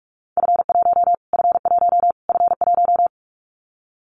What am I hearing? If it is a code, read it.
Morse code, heard as a keyed tone, about 44 wpm, 700 Hz. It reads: F1 F1 F1
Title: Callsign Prefixes for Top 25 DXCC - Repeated 3x - 45wpm